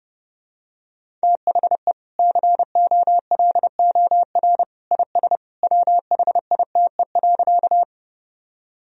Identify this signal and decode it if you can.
Morse code, heard as a keyed tone, about 30 wpm, 700 Hz. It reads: THE COLOR IS WHITE.
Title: Sentences from Top 1000 Words - Review Part 2 - 30wpm